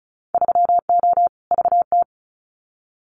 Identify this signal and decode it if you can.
Morse code, heard as a keyed tone, about 34 wpm, 700 Hz. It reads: 3O VT
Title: Winter Field Day Exchanges Encore 35wpm